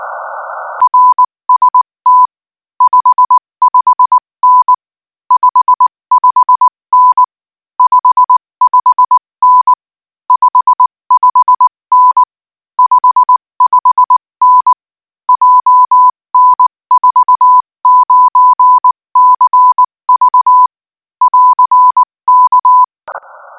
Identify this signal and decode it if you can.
Morse code, heard as a keyed tone, about 19 wpm, 1000 Hz. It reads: RST 55N 55N 55N 55N 55N JN49CV <AR> K
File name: IZ4BEH-13-cm-AS